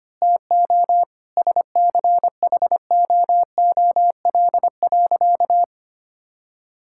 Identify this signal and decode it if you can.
Morse code, heard as a keyed tone, about 25 wpm, 700 Hz. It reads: TO SCHOOL.
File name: practice / 25wpm / Sentences with 3-Word Phrases - 25wpm